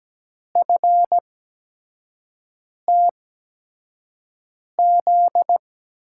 Morse code, heard as a keyed tone, about 17 wpm, spaced as for 5 wpm, 700 Hz. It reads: FTZ
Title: USA Call Signs - 6 Characters - 8x Character Spacing - 17wpm